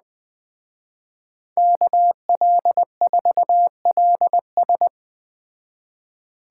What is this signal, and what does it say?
Morse code, heard as a keyed tone, about 20 wpm, 700 Hz. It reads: KL4LS
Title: Speed Racing - USA 2x2 Call Signs 17wpm